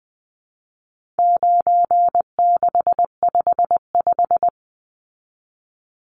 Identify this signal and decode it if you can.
Morse code, heard as a keyed tone, about 20 wpm, 700 Hz. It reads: 9655
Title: Numbers - 4-Digit 20wpm